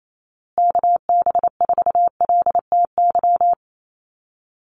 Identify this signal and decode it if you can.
Morse code, heard as a keyed tone, about 28 wpm, 700 Hz. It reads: KB4LTY
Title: USA Call Signs Repeated 2x 28wpm